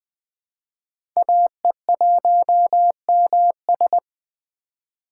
Morse code, heard as a keyed tone, about 20 wpm, 700 Hz. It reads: AE1MS
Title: 2023 SOTA Chaser Call Signs - 20wpm